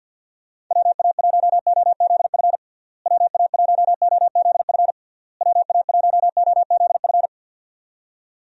Morse code, heard as a keyed tone, about 50 wpm, 700 Hz. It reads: WA1OZF WA1OZF WA1OZF
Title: USA Call Signs - 6 Characters - Repeated 3x - 50wpm